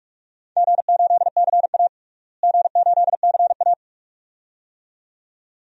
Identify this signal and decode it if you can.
Morse code, heard as a keyed tone, about 46 wpm, 700 Hz. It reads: G8CA G8CA
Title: DXCC-100 Call Signs - 4 Characters - Repeated 2x - 45wpm